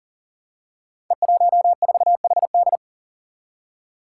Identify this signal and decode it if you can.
Morse code, heard as a keyed tone, about 42 wpm, 700 Hz. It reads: E14HD
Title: Five Challenging Letter-Number Combinations 40wpm